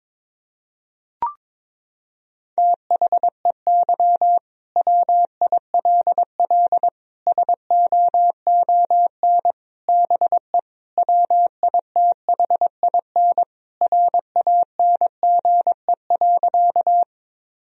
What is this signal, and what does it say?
Morse code, heard as a keyed tone, about 21 wpm, 700 Hz. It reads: THEY WILL SOON BE WITHIN RANGE.
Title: Sentences from Top 1000 Words - Review Part 4 - 22wpm